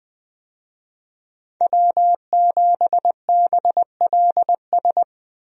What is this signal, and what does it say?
Morse code, heard as a keyed tone, about 20 wpm, 700 Hz. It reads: W7BLS
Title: USA Call Signs - 5 Characters - 20wpm